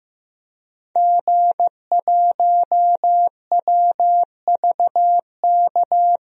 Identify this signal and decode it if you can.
Morse code, heard as a keyed tone, about 15 wpm, 700 Hz. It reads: G1WVK
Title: DXCC-100 Call Signs - 5 Characters - 15wpm